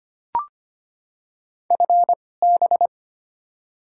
Morse code, heard as a keyed tone, about 25 wpm, spaced as for 18 wpm, 700 Hz. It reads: FB
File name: Top 20 QSO Elements - 2x Character Spacing - 25wpm